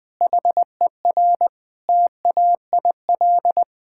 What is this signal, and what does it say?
Morse code, heard as a keyed tone, about 20 wpm, 700 Hz. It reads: HER TAIL
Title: Sentences with 2-Word Phrases - 20wpm